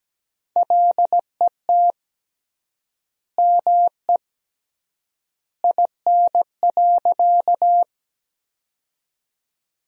Morse code, heard as a keyed tone, about 17 wpm, 700 Hz. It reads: LET ME IN.
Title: 3x Word-Spacing - Sentences from Top 300 Words - 17wpm